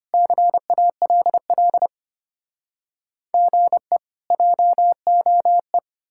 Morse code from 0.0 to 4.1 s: CALL GE

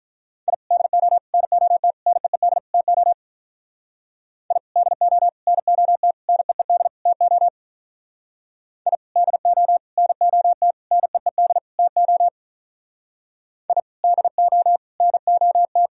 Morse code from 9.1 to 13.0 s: DO NOT NEED TO

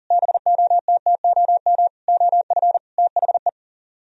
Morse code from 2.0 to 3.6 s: OF THE

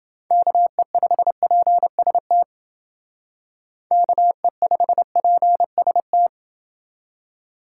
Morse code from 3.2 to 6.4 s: KE5PST

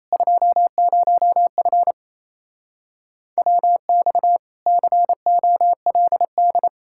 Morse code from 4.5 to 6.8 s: COLD

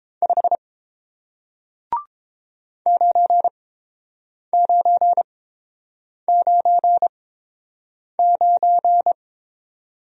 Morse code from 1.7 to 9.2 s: 9999

